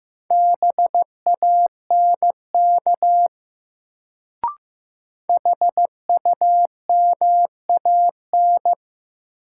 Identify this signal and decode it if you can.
Morse code, heard as a keyed tone, about 15 wpm, 700 Hz. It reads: BANK HUMAN
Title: Sets of 5 Words - New Word Drill - 15wpm